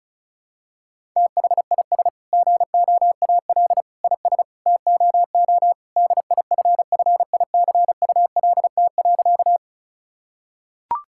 Morse code, heard as a keyed tone, about 36 wpm, 700 Hz. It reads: THIS GOAL IS TOO DIFFICULT.
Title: Sentences from Top 1000 Words - Review Part 5 - 35wpm